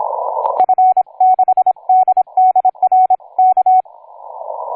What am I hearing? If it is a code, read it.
Morse code, heard as a keyed tone, about 26 wpm, 740 Hz. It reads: F6DDRK
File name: cw forum / yk9sv 1820 23h45